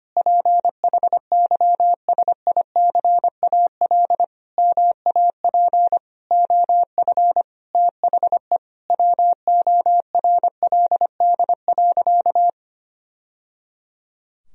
Morse code, heard as a keyed tone, about 25 wpm, 700 Hz. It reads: PHYSICAL MAP OF THE WORLD.
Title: Sentences from Top 1000 Words - Review Part 4 - 25wpm